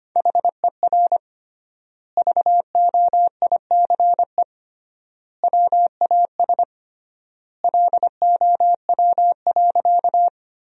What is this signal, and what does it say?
Morse code, heard as a keyed tone, about 25 wpm, 700 Hz. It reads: HER VOICE WAS LOW.
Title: 3x Word-Spacing - Sentences from Top 500 Words - 25wpm